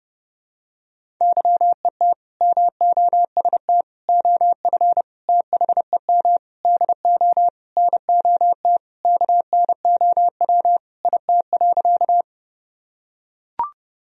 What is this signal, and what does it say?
Morse code, heard as a keyed tone, about 30 wpm, 700 Hz. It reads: YET MOST OF THEM DO NOT KNOW IT.
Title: Sentences from Top 1000 Words - Review Part 1 - 30wpm